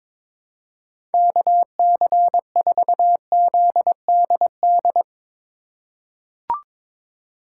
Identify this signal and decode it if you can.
Morse code, heard as a keyed tone, about 21 wpm, 700 Hz. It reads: KC4ZDD
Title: USA Call Signs - 6 Characters - 3x Character Spacing - 22wpm